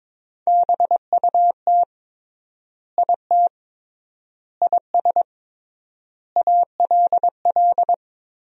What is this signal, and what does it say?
Morse code, heard as a keyed tone, about 22 wpm, 700 Hz. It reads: BUT IT IS ALL
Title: 3x Word-Spacing - Sentences from Top 300 Words - 22wpm